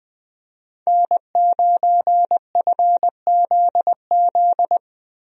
Morse code, heard as a keyed tone, about 20 wpm, 700 Hz. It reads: N9FZZ
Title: USA Call Signs - 5 Characters - Repeated 3x - 20wpm